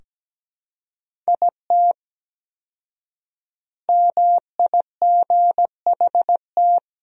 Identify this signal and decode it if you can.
Morse code, heard as a keyed tone, about 17 wpm, 700 Hz. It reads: IT MIGHT
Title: 4x Word-Spacing - Sentences from Top 300 Words - 17wpm